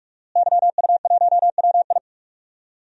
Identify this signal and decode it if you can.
Morse code, heard as a keyed tone, about 45 wpm, 700 Hz. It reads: YU1WI
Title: DXCC-100 Call Signs - 5 Characters - Repeated 2x - 45wpm